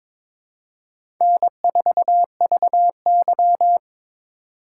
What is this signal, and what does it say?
Morse code, heard as a keyed tone, about 22 wpm, 700 Hz. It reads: N4VY
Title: USA Call Signs - 4 Characters - 4x Character Spacing - 22wpm